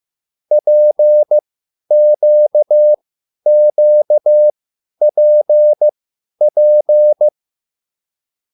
Morse code, heard as a keyed tone, about 15 wpm, spaced as for 10 wpm, 600 Hz. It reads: PQQPP
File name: QSL / CW lesson